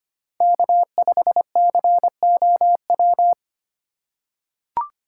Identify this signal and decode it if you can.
Morse code, heard as a keyed tone, about 25 wpm, 700 Hz. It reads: K5COW
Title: USA Call Signs - 2x Character Spacing - 25wpm